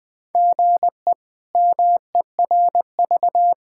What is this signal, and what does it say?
Morse code, heard as a keyed tone, about 21 wpm, 700 Hz. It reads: GE MERV
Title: K1USN SST QSO Practice - 20wpm